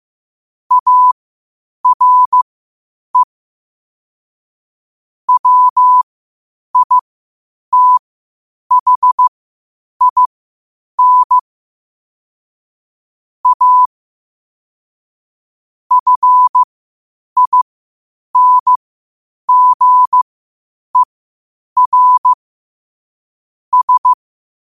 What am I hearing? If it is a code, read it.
Morse code, heard as a keyed tone, about 15 wpm, spaced as for 8 wpm, 1000 Hz. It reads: ARE WITHIN A FINGER S
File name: qotdfiles / 8wpm